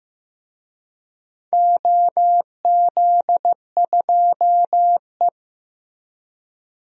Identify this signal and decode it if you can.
Morse code, heard as a keyed tone, about 15 wpm, 700 Hz. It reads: OZ2E
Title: DXCC-100 Call Signs - 4 Characters - Repeated 2x - 15wpm